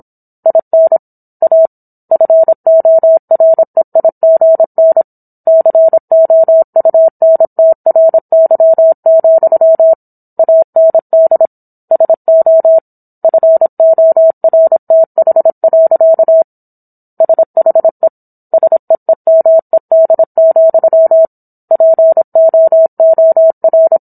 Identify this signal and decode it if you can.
Morse code, heard as a keyed tone, about 27 wpm, 650 Hz. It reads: IN A FOREIGN COUNTRY, AND SO FORTH. SHE SEEMED, POOR